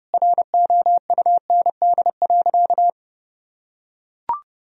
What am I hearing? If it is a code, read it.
Morse code, heard as a keyed tone, about 30 wpm, 700 Hz. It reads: ROUND.
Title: Sentences from Top 1000 Words - Review Part 1 - 30wpm